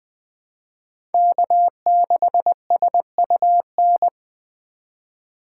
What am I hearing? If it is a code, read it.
Morse code, heard as a keyed tone, about 20 wpm, 700 Hz. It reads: K6SUN